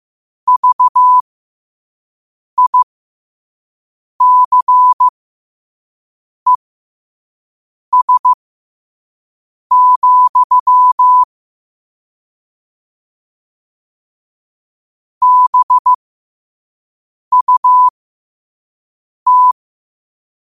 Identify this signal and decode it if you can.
Morse code, heard as a keyed tone, about 15 wpm, spaced as for 5 wpm, 1000 Hz. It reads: VICES, BUT
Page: qotd-at-5-wpm